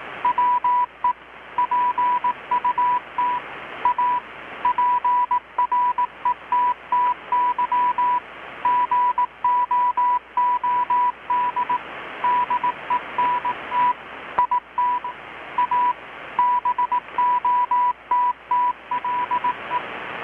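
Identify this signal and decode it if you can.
Morse code, heard as a keyed tone, about 18 wpm, 980 Hz. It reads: WE PUT A PRETTY GOOD DENT IT A BOTTLE
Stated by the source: Morse from KPH's Night of Nights XX maritime radiotelegraphy event, received in Chiapas, Mexico about 3,800 km away on a Kenwood R-600 with a grounded 10 m long-wire antenna.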